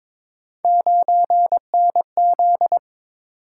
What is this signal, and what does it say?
Morse code, heard as a keyed tone, about 22 wpm, 700 Hz. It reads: 9NZ
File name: practice / 22wpm / Three Letter-Number Combinations - 22wpm